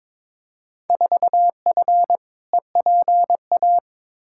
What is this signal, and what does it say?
Morse code, heard as a keyed tone, about 22 wpm, 700 Hz. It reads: 4F EPA
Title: Field Day Exchanges - 22wpm